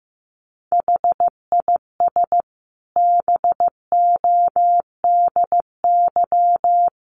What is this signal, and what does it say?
Morse code, heard as a keyed tone, about 15 wpm, 700 Hz. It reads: HIS BODY
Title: A Sound Of Thunder 15wpm